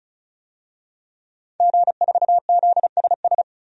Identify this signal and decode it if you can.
Morse code, heard as a keyed tone, about 36 wpm, 700 Hz. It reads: G4ZSS
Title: DXCC-100 Call Signs - 5 Characters - Repeated 2x - 35wpm